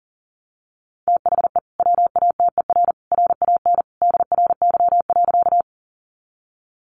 Morse code, heard as a keyed tone, about 40 wpm, 700 Hz. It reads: THE WATER RAN DRY.
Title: Sentences from Top 500 Words 40wpm